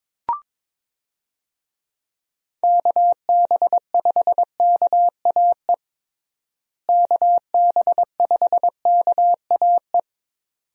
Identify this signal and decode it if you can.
Morse code, heard as a keyed tone, about 22 wpm, 700 Hz. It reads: KB5KAE KB5KAE
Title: USA Call Signs - 6 Characters - Repeated 2x - 22wpm